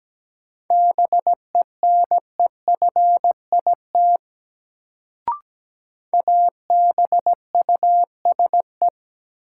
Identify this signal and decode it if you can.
Morse code, heard as a keyed tone, about 17 wpm, 700 Hz. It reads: BENEFIT ABUSE
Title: Sentences from Top 1300 Words - New Word Drill Alt - 17wpm